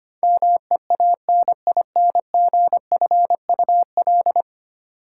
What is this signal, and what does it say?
Morse code, heard as a keyed tone, about 25 wpm, 700 Hz. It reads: MEANINGFUL
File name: Ten Letter Words - 25wpm